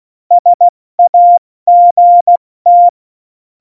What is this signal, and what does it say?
Morse code, heard as a keyed tone, about 16 wpm, 700 Hz. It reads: SAGT
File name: Morsekurs / Witze / Witz 20. November 2018, 70 F80